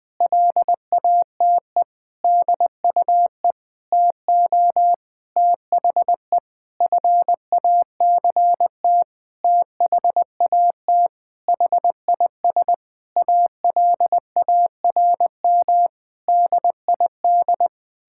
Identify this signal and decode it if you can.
Morse code, heard as a keyed tone, about 21 wpm, 700 Hz. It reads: LATE DUE TO THE FACT THAT HIS ALARM DID
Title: Sentences with 4-Word Phrases - Spoken Beginning - 20wpm